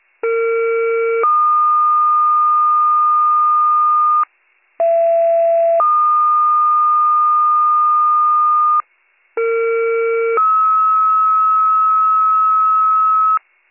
Two-tone page, heard as a one-second tone, then a three-second tone, about 460 then 1220 Hz, about 670 then 1220 Hz, about 460 then 1290 Hz. It is heard from 0.2 to 13.4 s.